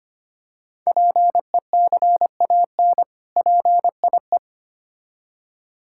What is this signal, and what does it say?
Morse code, heard as a keyed tone, about 25 wpm, 700 Hz. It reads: PECAN PIE
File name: practice / 25wpm / Difficult Words with Hints - 25wpm